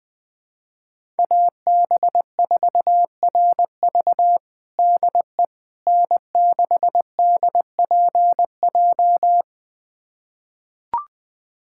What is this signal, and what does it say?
Morse code, heard as a keyed tone, about 20 wpm, 700 Hz. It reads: AB4RV DE N6DPJ
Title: USA Callsign DE Callsign - 5 Characters - 20wpm